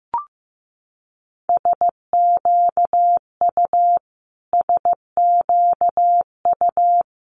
Morse code, heard as a keyed tone, about 15 wpm, 700 Hz. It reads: SQU SQU
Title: Multi-letter Phonemes Repeated 3x 15wpm